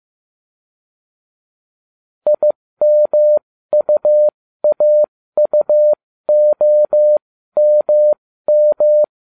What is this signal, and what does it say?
Morse code, heard as a keyed tone, about 15 wpm, spaced as for 13 wpm, 600 Hz. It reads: IMUAUOMM